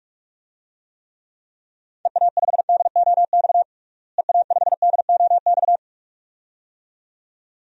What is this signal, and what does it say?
Morse code, heard as a keyed tone, about 46 wpm, 700 Hz. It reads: EA5DOX EA5DOX
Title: DXCC-100 Call Signs - 6 Characters - Repeated 2x - 45wpm